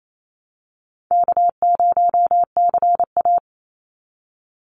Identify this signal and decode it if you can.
Morse code, heard as a keyed tone, about 28 wpm, 700 Hz. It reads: K0CA
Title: Easy Call Signs Repeated 2x - 28wpm